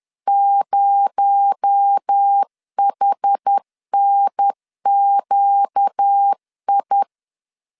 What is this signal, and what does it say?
Morse code, heard as a keyed tone, about 10 wpm, 800 Hz. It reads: 0HNQI
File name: CW / DET / 1 9 1500C S B 10wpm 4439